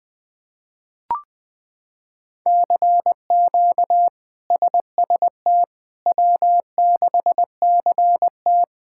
Morse code, heard as a keyed tone, about 20 wpm, 700 Hz. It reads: CQ SST W6CT
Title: K1USN SST QSO Practice - 20wpm